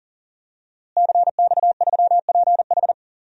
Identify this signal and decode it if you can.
Morse code, heard as a keyed tone, about 40 wpm, 700 Hz. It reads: CX3PH
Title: DXCC-100 Call Signs - 5 Characters - Repeated 3x - 40wpm